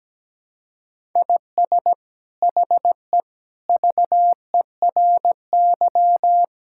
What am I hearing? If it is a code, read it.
Morse code, heard as a keyed tone, about 17 wpm, 700 Hz. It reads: IS HE VERY
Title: Sentences from Top 1000 Words - Review Part 5 - 17wpm